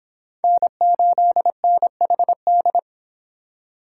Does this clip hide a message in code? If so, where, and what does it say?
Yes: Morse code from 0.4 to 2.8 s, heard as a keyed tone, about 25 wpm, 700 Hz. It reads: N8NHD